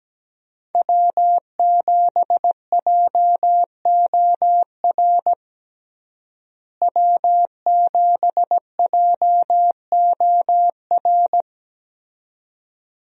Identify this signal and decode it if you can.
Morse code, heard as a keyed tone, about 17 wpm, 700 Hz. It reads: W7JOR W7JOR
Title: USA Call Signs - 5 Characters - Repeated 2x - 17wpm